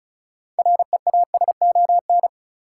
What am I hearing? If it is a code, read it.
Morse code, heard as a keyed tone, about 36 wpm, 700 Hz. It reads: REASON
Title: Sentences from Top 700 Words - New Word Drill - 35wpm